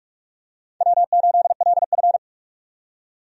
Morse code, heard as a keyed tone, about 46 wpm, 700 Hz. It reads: W8LF
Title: USA Call Signs - 4 Characters - 45wpm